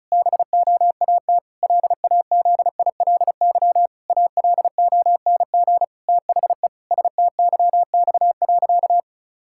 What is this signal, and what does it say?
Morse code, heard as a keyed tone, about 36 wpm, 700 Hz. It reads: BOAT LAZILY ALONG THE STYX.
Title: Pangram Challenge v2 - 35wpm